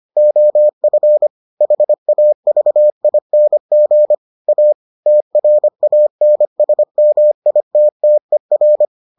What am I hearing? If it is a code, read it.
Morse code, heard as a keyed tone, about 25 wpm, 600 Hz. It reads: OF HAVING A TRANSMITTER